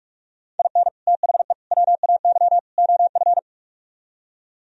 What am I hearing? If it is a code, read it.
Morse code, heard as a keyed tone, about 49 wpm, 700 Hz. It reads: IN THE WAY OF